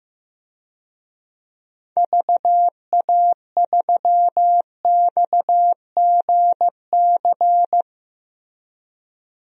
Morse code, heard as a keyed tone, about 15 wpm, 700 Hz. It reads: VA3XGC
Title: DXCC-100 Call Signs - 6 Characters - Repeated 3x - 15wpm